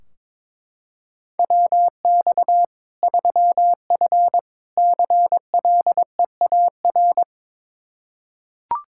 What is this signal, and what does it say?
Morse code, heard as a keyed tone, about 22 wpm, 700 Hz. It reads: WX 3F CLEAR
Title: K1USN SST QSO Practice 22wpm